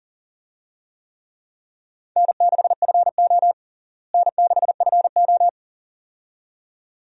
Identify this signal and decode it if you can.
Morse code, heard as a keyed tone, about 40 wpm, 700 Hz. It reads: N6FO N6FO
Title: USA Call Signs - 4 Characters - Repeated 2x - 40wpm